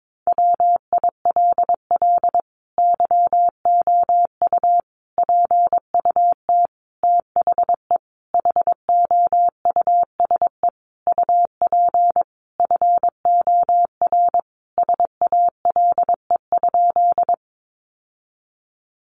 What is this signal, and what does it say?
Morse code, heard as a keyed tone, about 22 wpm, 700 Hz. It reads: WILL YOU PUT THE HOUSE UP FOR SALE?
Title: Sentences from Top 1400 Words 22wpm